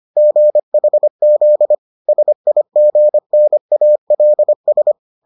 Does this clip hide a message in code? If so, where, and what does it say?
Yes: Morse code from 0.2 to 4.9 s, heard as a keyed tone, about 25 wpm, 600 Hz. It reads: GHZ SIGNALS